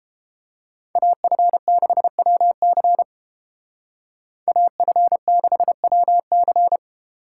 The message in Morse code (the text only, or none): AF6WC AF6WC